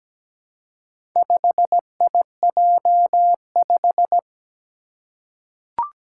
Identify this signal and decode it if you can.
Morse code, heard as a keyed tone, about 17 wpm, 700 Hz. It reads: 5IJ5
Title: Four Challenging Letter-Number Combinations 17wpm